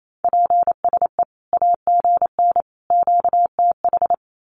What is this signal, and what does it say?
Morse code, heard as a keyed tone, about 28 wpm, 700 Hz. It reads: PSE AGN QTH
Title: Common QSO Phrases 28wpm